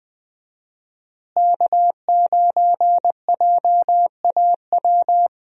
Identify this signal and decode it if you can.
Morse code, heard as a keyed tone, about 20 wpm, 700 Hz. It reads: K9JAW
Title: USA Call Signs - 5 Characters - Repeated 3x - 20wpm